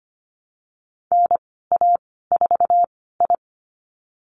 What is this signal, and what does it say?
Morse code, heard as a keyed tone, about 25 wpm, spaced as for 16 wpm, 690 Hz. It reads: NA4I
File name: Lesson 15g - Number 5 - Call Signs 25wpm